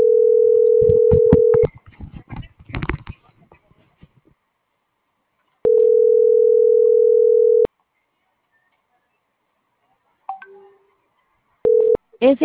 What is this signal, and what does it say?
Ringback tone (North America), 440 + 480 Hz.
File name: CENTRAL GENERAL / out alexander.sanchez